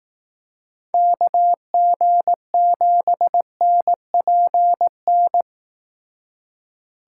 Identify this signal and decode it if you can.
Morse code, heard as a keyed tone, about 18 wpm, 700 Hz. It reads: KG7NPN
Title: Speed Racing - USA 2x3 Call Signs 15wpm